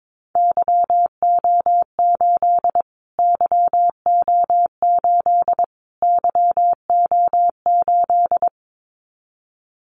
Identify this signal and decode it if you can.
Morse code, heard as a keyed tone, about 22 wpm, 700 Hz. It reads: YO8 YO8 YO8
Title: Callsign Prefixes for Top 25 DXCC - Repeated 3x - 22wpm